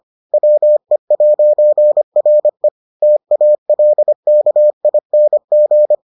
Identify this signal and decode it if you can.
Morse code, heard as a keyed tone, about 25 wpm, 600 Hz. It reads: WE'RE TALKING